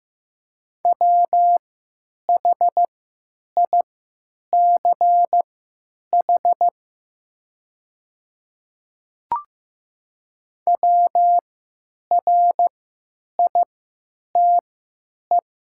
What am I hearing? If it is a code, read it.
Morse code, heard as a keyed tone, about 15 wpm, spaced as for 9 wpm, 700 Hz. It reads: WHICH WRITE
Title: 3x Character Spacing - Top 100 Words - 15wpm